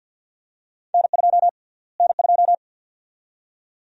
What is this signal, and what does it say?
Morse code, heard as a keyed tone, about 50 wpm, 700 Hz. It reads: N2 N2
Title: USA Call Sign Prefixes - Repeated 2x - 50wpm